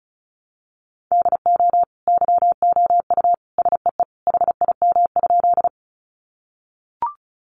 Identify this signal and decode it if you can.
Morse code, heard as a keyed tone, about 34 wpm, 700 Hz. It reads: DO YOU SEE HIM?
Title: Sentences from Top 100 Words 35wpm